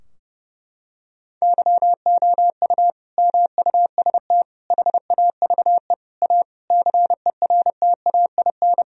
Morse code, heard as a keyed tone, about 32 wpm, 700 Hz. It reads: YOU MUST HAVE A CERTAIN
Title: Sentences from Top 1000 Words - Review Part 4 - 30wpm